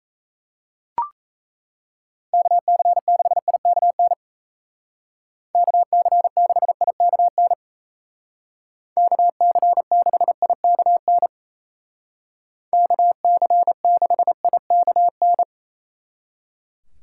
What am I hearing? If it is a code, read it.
Morse code, heard as a keyed tone, about 33 wpm, 700 Hz. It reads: KC6IKN KC6IKN KC6IKN KC6IKN
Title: Speed Racing - USA Call Signs 28wpm